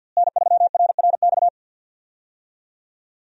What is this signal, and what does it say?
Morse code, heard as a keyed tone, about 50 wpm, 700 Hz. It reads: N3RRX
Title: USA Call Signs - 5 Characters - Repeated 2x - 50wpm